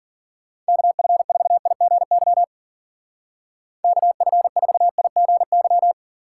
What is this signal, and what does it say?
Morse code, heard as a keyed tone, about 52 wpm, 700 Hz. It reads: KF4IGY KF4IGY